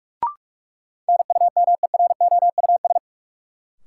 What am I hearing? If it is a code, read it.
Morse code, heard as a keyed tone, about 43 wpm, 700 Hz. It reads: NUMEROUS
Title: Sentences from Top 1400 Words - New Word Drill - 45wpm